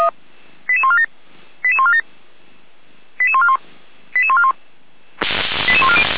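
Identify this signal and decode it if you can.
Five-tone selective call (ZVEI), heard as quick runs of five tones, tones of about 70 ms. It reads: 88147 88147 88141 88141 88140
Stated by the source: ZVEI five-tone selective call received by SDR and recorded by the fire brigade (Feuerwehr) Orlamünde.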